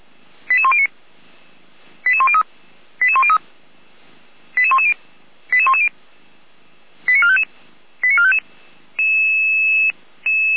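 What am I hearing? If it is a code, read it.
Five-tone selective call (ZVEI), heard as quick runs of five tones, tones of about 70 ms. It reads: 88109 88103 88103 88110 88110 89455 89455
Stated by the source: ZVEI five-tone call received on SDR, from feuerwehr-orlamuende fire brigade audio server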